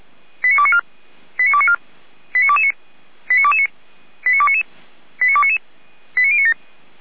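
Five-tone selective call (ZVEI), heard as quick runs of five tones, tones of about 70 ms. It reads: 89294 89294 89229 89229 89299 89299 89097
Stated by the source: ZVEI five-tone call 89276 received by SDR, from the feuerwehr-orlamuende.org fire brigade server.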